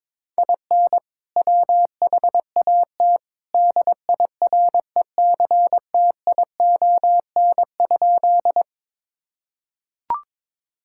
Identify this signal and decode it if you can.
Morse code, heard as a keyed tone, about 22 wpm, 700 Hz. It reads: IN WHAT DIRECTION?
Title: Sentences from Top 1000 Words - Review Part 5 - 22wpm